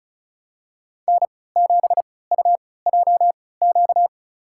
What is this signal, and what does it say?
Morse code, heard as a keyed tone, about 36 wpm, spaced as for 20 wpm, 700 Hz. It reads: N7UJQ
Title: USA Call Signs - 5 Characters - 3x Character Spacing - 35wpm